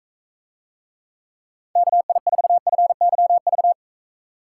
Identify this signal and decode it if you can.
Morse code, heard as a keyed tone, about 44 wpm, 700 Hz. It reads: KI4FYV